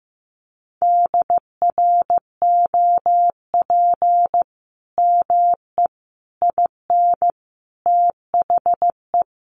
Morse code, heard as a keyed tone, about 14 wpm, 700 Hz. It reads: DROP ME IN THE
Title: Sentences from Top 1400 Words 15wpm